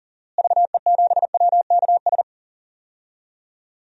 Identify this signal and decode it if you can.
Morse code, heard as a keyed tone, about 40 wpm, 700 Hz. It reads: VE7WKS
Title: DXCC-100 Call Signs - 6 Characters - Repeated 3x - 40wpm